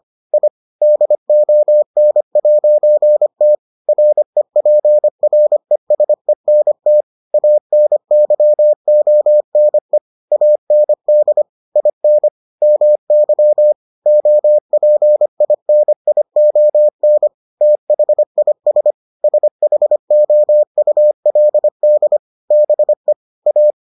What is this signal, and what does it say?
Morse code, heard as a keyed tone, about 25 wpm, 600 Hz. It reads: I DON'T REPRESENT ANYONE AND IN MY OPINION THIS SHOULD BE A